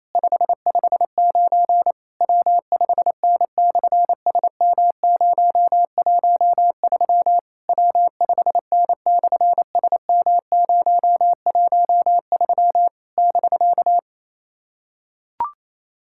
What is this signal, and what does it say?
Morse code, heard as a keyed tone, about 27 wpm, 700 Hz. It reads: 559 W5N/SM013 W5N/SM013 <BK>
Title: SOTA QSO Practice 28wpm